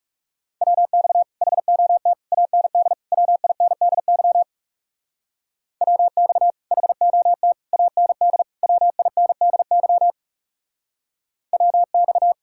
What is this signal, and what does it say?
Morse code, heard as a keyed tone, about 35 wpm, 700 Hz. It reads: WX HOT AND WINDY WX HOT AND WINDY WX
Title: Speed-Racing Common QSO Phrases - 30wpm